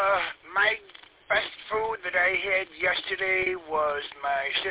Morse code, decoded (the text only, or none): none